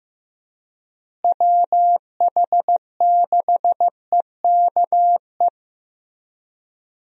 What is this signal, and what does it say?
Morse code, heard as a keyed tone, about 15 wpm, 700 Hz. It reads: WH6EKE
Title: USA Call Signs - 6 Characters - Repeated 2x - 15wpm